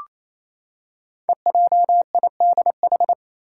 Morse code, heard as a keyed tone, about 27 wpm, 700 Hz. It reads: EJIDH
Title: Code Groups - 5 Letters - 28wpm